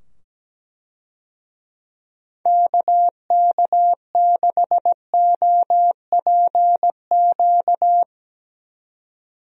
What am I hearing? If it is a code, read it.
Morse code, heard as a keyed tone, about 17 wpm, 700 Hz. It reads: KK6OPQ
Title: USA Call Signs - 6 Characters - 17wpm